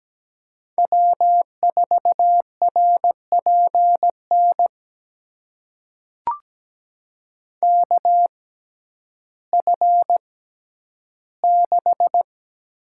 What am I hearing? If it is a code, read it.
Morse code, heard as a keyed tone, about 17 wpm, spaced as for 10 wpm, 700 Hz. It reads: W4RPN KF6
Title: USA Call Signs - 6x Character Spacing - 17wpm